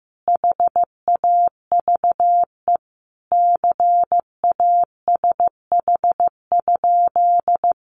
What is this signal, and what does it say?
Morse code, heard as a keyed tone, about 15 wpm, 700 Hz. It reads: HAVE CASH?
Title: Sentences from Top 700 Words 15wpm